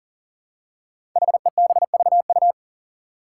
Morse code, heard as a keyed tone, about 40 wpm, 700 Hz. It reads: HEBVU
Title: Five Challenging Letter-Number Combinations 40wpm